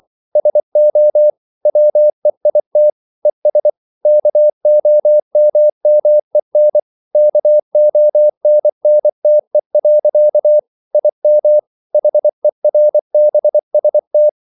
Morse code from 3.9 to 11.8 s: KOMMEN KONNTE. IM